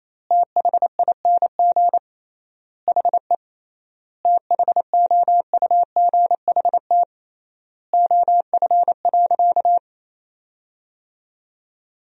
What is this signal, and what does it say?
Morse code, heard as a keyed tone, about 28 wpm, 700 Hz. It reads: THING HE THOUGHT OF.
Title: 3x Word-Spacing - Sentences from Top 300 Words - 28wpm